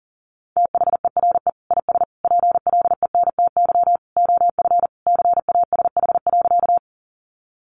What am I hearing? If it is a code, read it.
Morse code, heard as a keyed tone, about 40 wpm, 700 Hz. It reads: THERE IS PLENTY OF CASH.
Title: Sentences from Top 1300 Words 40wpm